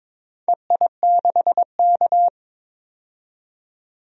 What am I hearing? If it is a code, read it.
Morse code, heard as a keyed tone, about 22 wpm, 700 Hz. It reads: EI6K
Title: DXCC-100 Call Signs - 4 Characters - 22wpm